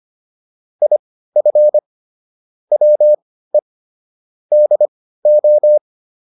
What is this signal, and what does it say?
Morse code, heard as a keyed tone, about 25 wpm, spaced as for 15 wpm, 600 Hz. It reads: IF WE DO